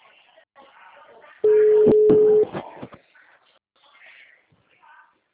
Ringback tone (Europe, 425 Hz).